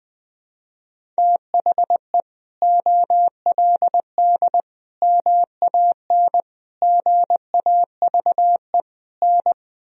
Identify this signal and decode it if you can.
Morse code, heard as a keyed tone, about 20 wpm, 700 Hz. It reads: THE OLD MAN GAVE N
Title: Sentences from Top 500 Words and Abbreviations 20wpm